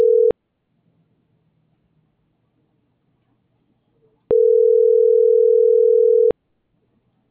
Ringback tone (North America), 440 + 480 Hz.